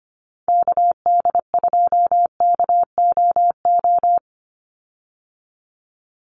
Morse code, heard as a keyed tone, about 25 wpm, 700 Hz. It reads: KD2KOO
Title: USA Call Signs 25wpm